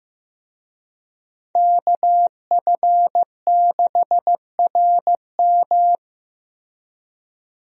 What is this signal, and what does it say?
Morse code, heard as a keyed tone, about 15 wpm, 700 Hz. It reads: KF6RM
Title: USA Call Signs - 5 Characters - Repeated 3x - 15wpm